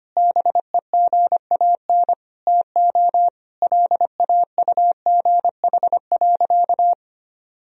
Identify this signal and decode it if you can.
Morse code, heard as a keyed tone, about 25 wpm, 700 Hz. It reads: BEGAN TO LAUGH.
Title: Sentences from Top 1000 Words - Review Part 2 - 25wpm